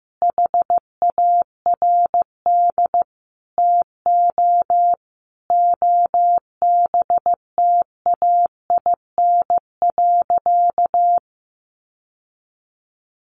Morse code, heard as a keyed tone, about 15 wpm, 700 Hz. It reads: HARD TO OBTAIN.
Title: Sentences from Top 1300 Words 15wpm